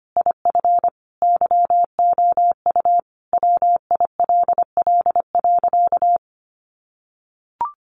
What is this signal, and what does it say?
Morse code, heard as a keyed tone, about 25 wpm, 700 Hz. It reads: IF YOU WILL.
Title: Sentences from Top 1000 Words 25wpm